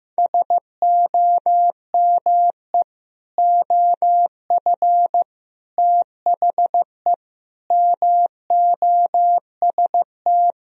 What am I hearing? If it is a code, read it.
Morse code, heard as a keyed tone, about 15 wpm, 700 Hz. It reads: SOME OF THE MOST
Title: Speed-Racing - Sets of 4 Words 15wpm